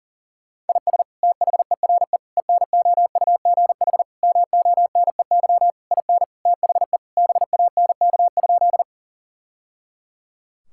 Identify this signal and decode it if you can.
Morse code, heard as a keyed tone, about 40 wpm, 700 Hz. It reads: IS THERE ENOUGH MONEY IN THE BANK?
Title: Sentences from Top 1000 Words - Review Part 6 - 40wpm